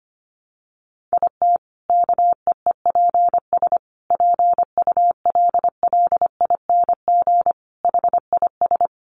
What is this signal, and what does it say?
Morse code, heard as a keyed tone, about 25 wpm, 700 Hz. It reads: IT KEEPS PULLING HIS
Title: Additional Easy Phrases 25wpm